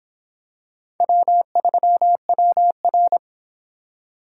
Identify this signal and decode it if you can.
Morse code, heard as a keyed tone, about 26 wpm, 700 Hz. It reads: W3WR